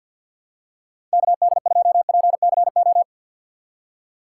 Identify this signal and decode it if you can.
Morse code, heard as a keyed tone, about 50 wpm, 700 Hz. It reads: KD2PCO
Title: USA Call Signs - 6 Characters - 6x Character Spacing - 50wpm